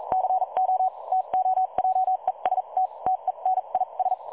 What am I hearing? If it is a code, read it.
Morse code, heard as a keyed tone, about 43 wpm, spaced as for 34 wpm, 750 Hz. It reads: CO TO JEST TENIS
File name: dlazaawansowanych / nagraniazpasma / websdr recording 3565.0kHz